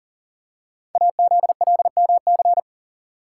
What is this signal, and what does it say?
Morse code, heard as a keyed tone, about 38 wpm, 690 Hz. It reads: AZLMC
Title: Code Groups - 5 Letters - 40wpm